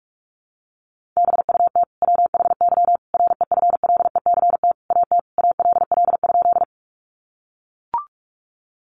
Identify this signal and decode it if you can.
Morse code, heard as a keyed tone, about 44 wpm, 700 Hz. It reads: BUT WHY REFLECT AT ALL?
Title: Sentences from Top 1500 Words 45wpm